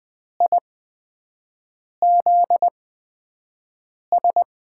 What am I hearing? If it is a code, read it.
Morse code, heard as a keyed tone, about 20 wpm, spaced as for 5 wpm, 700 Hz. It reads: IZS
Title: USA Call Signs - 6 Characters - 8x Character Spacing - 20wpm